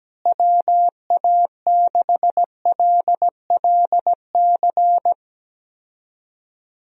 Morse code, heard as a keyed tone, about 17 wpm, 700 Hz. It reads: WA6LLC
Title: USA Call Signs - 6 Characters - Repeated 3x - 17wpm